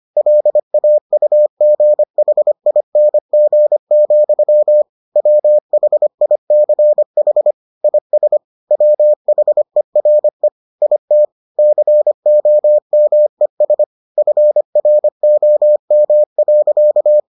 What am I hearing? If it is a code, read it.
Morse code, heard as a keyed tone, about 25 wpm, 600 Hz. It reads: LAUGHING, WHICH IS WHERE IT COMES FROM.